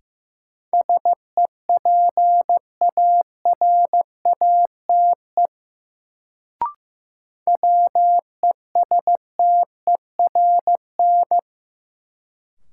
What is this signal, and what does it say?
Morse code, heard as a keyed tone, about 15 wpm, 700 Hz. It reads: SEPARATE WESTERN
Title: Sentences from Top 1000 Words - Review Part 7 - New Word Drill - 15wpm